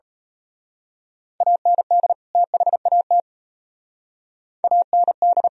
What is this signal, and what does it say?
Morse code, heard as a keyed tone, about 32 wpm, 700 Hz. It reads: AND THAT AND
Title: Speed-Racing - Sets of 2 Words 28wpm